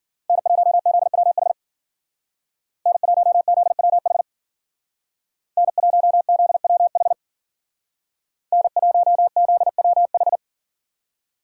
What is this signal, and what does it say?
Morse code, heard as a keyed tone, about 50 wpm, 700 Hz. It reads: N1ZWH N1ZWH N1ZWH N1ZWH
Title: Speed Racing - USA Call Signs 40wpm